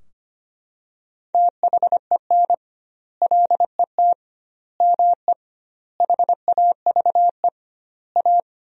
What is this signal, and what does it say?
Morse code, heard as a keyed tone, about 25 wpm, 700 Hz. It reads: THEN LET ME HAVE A
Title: 2x Word-Spacing - Sentences from Top 300 Words - 25wpm